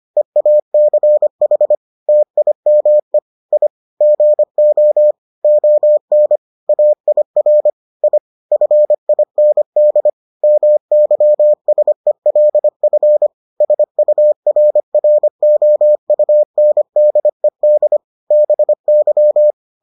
Morse code, heard as a keyed tone, about 25 wpm, 600 Hz. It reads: EACH TIME I GO ON AIR I FIND MYSELF SURROUNDED BY